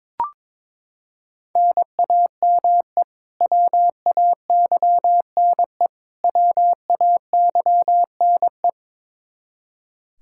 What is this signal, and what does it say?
Morse code, heard as a keyed tone, about 21 wpm, 700 Hz. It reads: NAME WAYNE WAYNE
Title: Practice QSO - Exchange 1 - Beginning - 22wpm